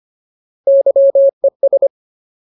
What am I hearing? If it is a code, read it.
Morse code, heard as a keyed tone, about 25 wpm, 550 Hz. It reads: YES
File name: Ultimate Word List / 25WPM Ultimate